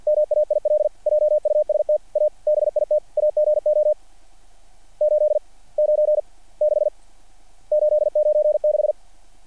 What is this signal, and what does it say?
Morse code, heard as a keyed tone, about 50 wpm, 660 Hz. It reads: MAIL JUST A BIT AGO 8 9 6 896
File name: archived / CRSnet